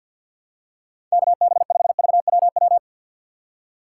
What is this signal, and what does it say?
Morse code, heard as a keyed tone, about 51 wpm, 700 Hz. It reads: KB5VWW
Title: USA Call Signs - 6 Characters - 4x Character Spacing - 50wpm